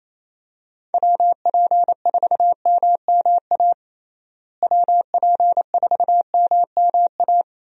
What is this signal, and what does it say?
Morse code, heard as a keyed tone, about 28 wpm, 700 Hz. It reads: WP4MMA WP4MMA
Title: USA Call Signs - 6 Characters - Repeated 2x - 28wpm